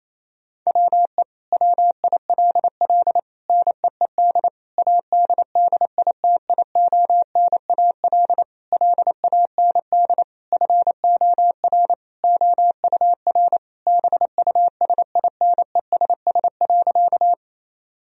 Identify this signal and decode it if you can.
Morse code, heard as a keyed tone, about 28 wpm, 700 Hz. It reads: WE WILL NEED ADDITIONAL LAND FOR OUR BUSINESS.
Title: Sentences from Top 1000 Words - Review Part 4 - 28wpm